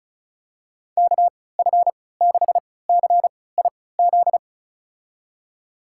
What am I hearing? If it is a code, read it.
Morse code, heard as a keyed tone, about 36 wpm, spaced as for 20 wpm, 700 Hz. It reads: KF6CIZ
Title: USA Call Signs - 6 Characters - 3x Character Spacing - 35wpm